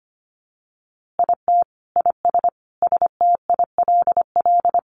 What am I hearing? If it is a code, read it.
Morse code, heard as a keyed tone, about 25 wpm, 700 Hz. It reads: IT IS STILL
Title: Easy Phrases 25wpm